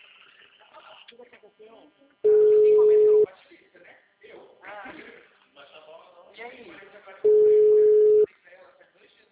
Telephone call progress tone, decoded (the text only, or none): ringback tone